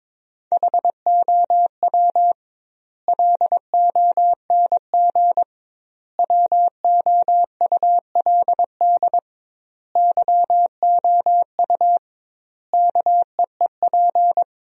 Morse code, heard as a keyed tone, about 22 wpm, 700 Hz. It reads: HOW LONG WOULD YOU KEEP
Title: 2x Word-Spacing - Sentences from Top 500 Words - 22wpm